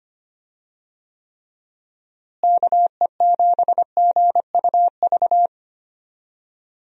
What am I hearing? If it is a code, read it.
Morse code, heard as a keyed tone, about 25 wpm, 700 Hz. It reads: KE7GUV
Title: USA Call Signs - 6 Characters - Repeated 2x - 25wpm